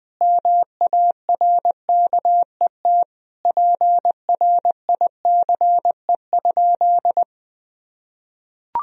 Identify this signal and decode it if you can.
Morse code, heard as a keyed tone, about 20 wpm, 700 Hz. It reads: MARKET PRICE?
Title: Sentences from Top 1000 Words - Review Part 3 - 20wpm